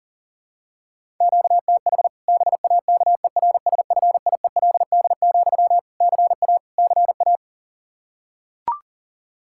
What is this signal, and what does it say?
Morse code, heard as a keyed tone, about 40 wpm, 700 Hz. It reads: QTH BAKERSFIELD, CA CA
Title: Practice QSO - Exchange 1 - Beginning - 40wpm